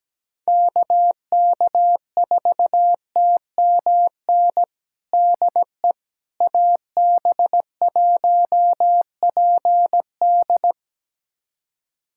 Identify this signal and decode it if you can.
Morse code, heard as a keyed tone, about 17 wpm, 700 Hz. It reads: KK4TMN DE AB1PD
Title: USA Callsign DE Callsign 17wpm